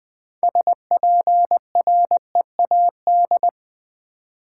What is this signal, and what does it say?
Morse code, heard as a keyed tone, about 20 wpm, 700 Hz. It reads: SPREAD
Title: 6 Letter Words - First 3 Letters Given - 20wpm